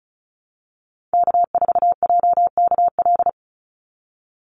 Morse code, heard as a keyed tone, about 34 wpm, 700 Hz. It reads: K4JKL
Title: Call Signs Repeated 2x - 35wpm